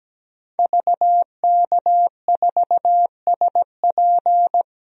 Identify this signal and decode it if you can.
Morse code, heard as a keyed tone, about 17 wpm, 700 Hz. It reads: VK4SP
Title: DXCC-100 Call Signs - 5 Characters - 17wpm